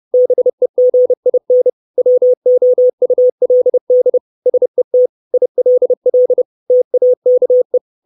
Morse code, heard as a keyed tone, about 30 wpm, 500 Hz. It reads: BEGIN WOULD SET ILL TAKE
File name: 632 Words 30WPM